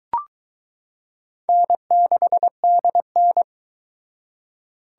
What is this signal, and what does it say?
Morse code, heard as a keyed tone, about 22 wpm, 700 Hz. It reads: N6DN